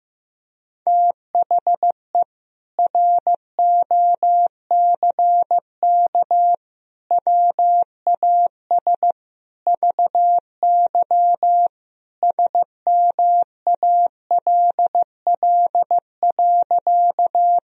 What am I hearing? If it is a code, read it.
Morse code, heard as a keyed tone, about 15 wpm, 700 Hz. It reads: THE ROCK WAS VY SMALL.
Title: Sentences from Top 400 Words and Abbreviations 15wpm